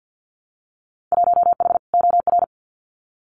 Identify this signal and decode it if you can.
Morse code, heard as a keyed tone, about 48 wpm, 700 Hz. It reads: 1H OR
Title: Winter Field Day Exchanges 50wpm